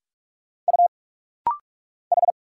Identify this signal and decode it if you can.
Morse code, heard as a keyed tone, about 45 wpm, 700 Hz. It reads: U H
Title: Single Letter-Number Alt - Rapid-Fire 45wpm